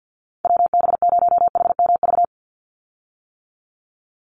Morse code, heard as a keyed tone, about 50 wpm, 700 Hz. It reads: WB0HMV
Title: USA Call Signs 50wpm